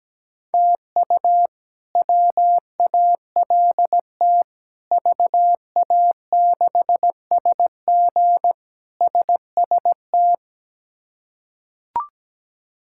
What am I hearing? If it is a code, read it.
Morse code, heard as a keyed tone, about 17 wpm, 700 Hz. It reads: TU WALT VA6SG SST
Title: K1USN SST QSO Practice - 17wpm